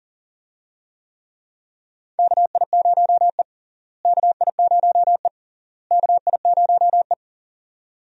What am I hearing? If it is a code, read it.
Morse code, heard as a keyed tone, about 40 wpm, 700 Hz. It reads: KI0E KI0E KI0E
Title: USA Call Signs - 4 Characters - Repeated 3x - 40wpm